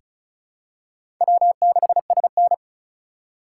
Morse code, heard as a keyed tone, about 36 wpm, 700 Hz. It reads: W6SN
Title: USA Call Signs - 3x Character Spacing - 35wpm